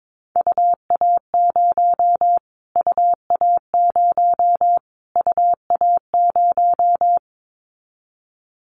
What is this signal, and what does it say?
Morse code, heard as a keyed tone, about 22 wpm, 700 Hz. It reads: UA0 UA0 UA0
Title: Callsign Prefixes for Top 25 DXCC - Repeated 3x - 22wpm